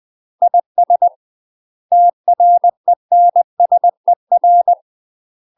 Morse code, heard as a keyed tone, about 20 wpm, 700 Hz. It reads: IS TRENSER
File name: UK MASTER QSOs TRAINING/20WPM